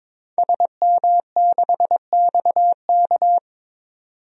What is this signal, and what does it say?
Morse code, heard as a keyed tone, about 22 wpm, 700 Hz. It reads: SM6XK
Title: DXCC-100 Call Signs - 5 Characters - 22wpm